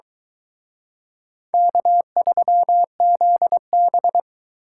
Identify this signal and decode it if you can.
Morse code, heard as a keyed tone, about 23 wpm, 700 Hz. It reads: K3ZB